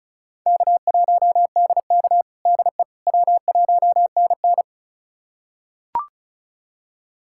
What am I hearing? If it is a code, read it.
Morse code, heard as a keyed tone, about 34 wpm, 700 Hz. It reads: K1DK DE W1NN
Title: USA Callsign DE Callsign - 4 Characters - 35wpm